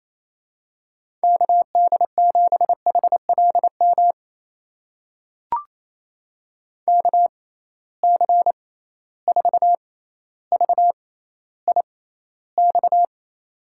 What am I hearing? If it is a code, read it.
Morse code, heard as a keyed tone, about 28 wpm, spaced as for 14 wpm, 700 Hz. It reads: KD7HLM KC4VIX
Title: USA Call Signs - 6 Characters - 6x Character Spacing - 28wpm